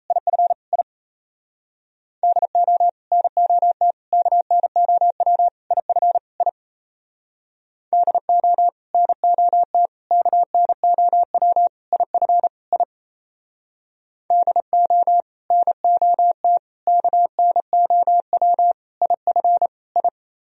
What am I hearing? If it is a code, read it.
Morse code, heard as a keyed tone, about 35 wpm, 700 Hz. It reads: IF I DO NOT KNOW IF I DO NOT KNOW IF I DO NOT KNOW IF I
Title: Speed-Racing - Sets of 5 Words 28wpm